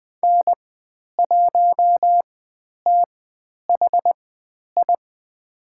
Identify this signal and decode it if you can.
Morse code, heard as a keyed tone, about 20 wpm, spaced as for 10 wpm, 700 Hz. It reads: N1THI
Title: 095 Lesson 16g - Letter c - Call Signs 20wpm